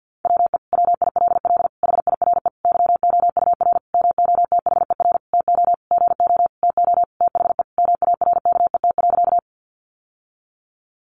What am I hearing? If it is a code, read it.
Morse code, heard as a keyed tone, about 50 wpm, 700 Hz. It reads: WE WILL HIRE YOUR MOTHER TO GO TO THE MARKET.
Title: Sentences from Top 1400 Words 50wpm